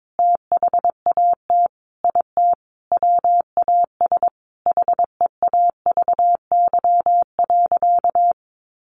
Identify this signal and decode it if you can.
Morse code, heard as a keyed tone, about 22 wpm, 700 Hz. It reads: THAT IT WAS HEAVY.
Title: Sentences from Top 1100 Words 22wpm